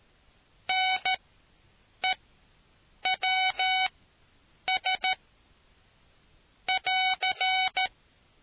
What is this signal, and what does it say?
Morse code, heard as a keyed tone, about 13 wpm, spaced as for 7 wpm, 750 Hz. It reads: NEWS <AR>